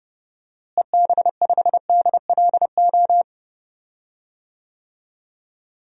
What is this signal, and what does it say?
Morse code, heard as a keyed tone, about 32 wpm, 700 Hz. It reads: EB5DLO
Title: DXCC-100 Call Signs - 6 Characters - Repeated 2x - 30wpm